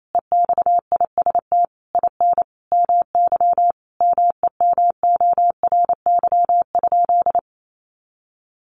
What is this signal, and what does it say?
Morse code, heard as a keyed tone, about 28 wpm, 700 Hz. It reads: EXIST IN MY MEMORY?
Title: Sentences from Top 1300 Words 28wpm